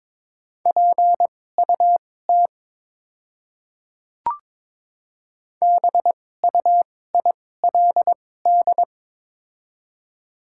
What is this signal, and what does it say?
Morse code, heard as a keyed tone, about 22 wpm, spaced as for 16 wpm, 700 Hz. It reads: PUT BUILD
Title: 2x Character Spacing - Top 300 Words - 22wpm